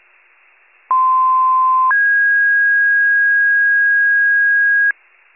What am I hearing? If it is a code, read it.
Two-tone page, heard as a one-second tone, then a three-second tone, about 1040 then 1650 Hz. It is heard from 0.9 to 4.9 s.
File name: pager / 95 RESCUE DUTY / wcalert